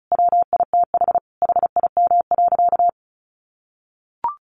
Morse code, heard as a keyed tone, about 34 wpm, 700 Hz. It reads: WITH HIM.
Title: Sentences from Top 1000 Words 35wpm